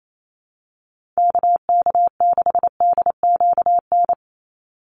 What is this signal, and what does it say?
Morse code, heard as a keyed tone, about 28 wpm, 700 Hz. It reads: KK6DQN
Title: USA Call Signs 28wpm